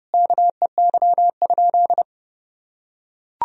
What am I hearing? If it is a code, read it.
Morse code, heard as a keyed tone, about 30 wpm, 700 Hz. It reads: KEY?
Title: Sentences from Top 1000 Words - Review Part 4 - 30wpm